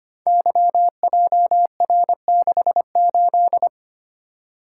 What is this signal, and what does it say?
Morse code, heard as a keyed tone, about 25 wpm, 700 Hz. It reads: YJR68
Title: Code Groups - 5 Letter:Numbers - 25wpm